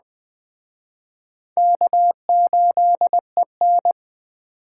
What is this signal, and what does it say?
Morse code, heard as a keyed tone, about 20 wpm, 700 Hz. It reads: K8EN
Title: Speed Racing - USA 1x2 Call Signs 17wpm